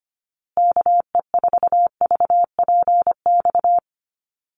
Morse code, heard as a keyed tone, about 25 wpm, 700 Hz. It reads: KE4VPX
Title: USA Call Signs Repeated 2x 25wpm